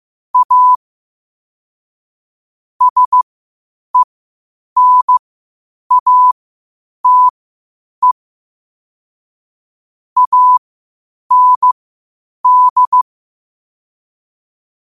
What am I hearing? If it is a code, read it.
Morse code, heard as a keyed tone, about 15 wpm, spaced as for 8 wpm, 1000 Hz. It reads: A SENATE AND
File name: qotdfiles / 8wpm